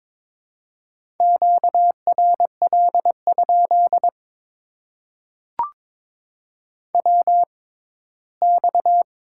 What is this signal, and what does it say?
Morse code, heard as a keyed tone, about 22 wpm, spaced as for 15 wpm, 700 Hz. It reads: QRL? WX
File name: Top 90 QSO Elements - 6x Character Spacing - 22wpm